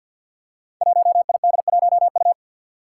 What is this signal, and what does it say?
Morse code, heard as a keyed tone, about 49 wpm, 700 Hz. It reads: 1ID1U
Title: Five Challenging Letter-Number Combinations 50wpm